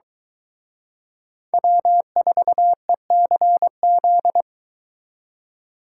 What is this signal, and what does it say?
Morse code, heard as a keyed tone, about 22 wpm, 700 Hz. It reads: W4ECZ